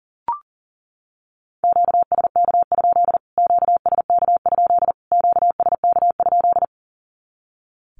Morse code, heard as a keyed tone, about 40 wpm, 700 Hz. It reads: QSK? QSK? QSK?
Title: Q-Codes Repeated 3x 40wpm